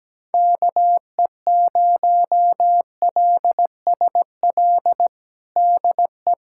Morse code, heard as a keyed tone, about 17 wpm, 700 Hz. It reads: KE0LSL DE
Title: USA Callsign DE Callsign 17wpm